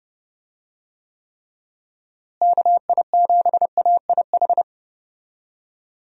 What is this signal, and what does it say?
Morse code, heard as a keyed tone, about 30 wpm, 700 Hz. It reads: KI7AIH
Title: USA Call Signs - 6 Characters - Repeated 3x - 30wpm